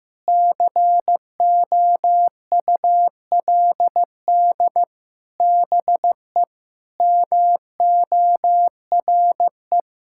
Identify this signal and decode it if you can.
Morse code, heard as a keyed tone, about 15 wpm, 700 Hz. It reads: COULD BE MORE
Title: Sentences from Top 1000 Words - Review Part 5 - 15wpm